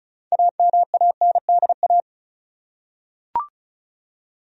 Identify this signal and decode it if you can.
Morse code, heard as a keyed tone, about 34 wpm, 700 Hz. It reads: AMANDA
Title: USA Common Names - 35wpm